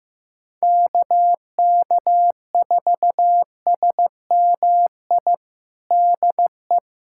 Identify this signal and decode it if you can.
Morse code, heard as a keyed tone, about 15 wpm, 700 Hz. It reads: KK4SMI DE
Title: USA Callsign DE Callsign - 6 Characters - 15wpm